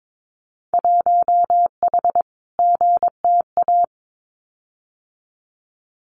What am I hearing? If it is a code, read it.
Morse code, heard as a keyed tone, about 22 wpm, 700 Hz. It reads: 1H GTA
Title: Winter Field Day Exchanges 22wpm